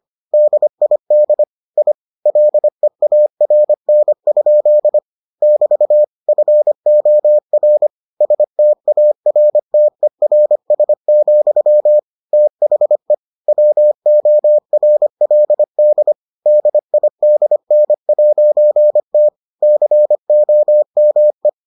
Morse code, heard as a keyed tone, about 25 wpm, 600 Hz. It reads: DID I LEARN? <BT> FOR STARTERS, THE WORLD DIDN'T COME